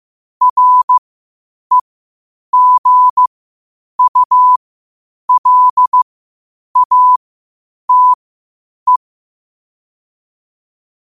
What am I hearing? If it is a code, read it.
Morse code, heard as a keyed tone, about 15 wpm, spaced as for 8 wpm, 1000 Hz. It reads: REGULATE
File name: qotdfiles / 8wpm